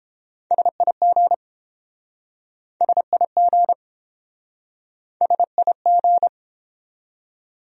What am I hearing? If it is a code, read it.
Morse code, heard as a keyed tone, about 29 wpm, 700 Hz. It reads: SIG SIG SIG